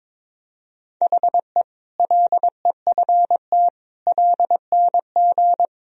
Morse code, heard as a keyed tone, about 22 wpm, 700 Hz. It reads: HE LEFT LNG
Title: Sentences from Top 500 Words and Abbreviations 22wpm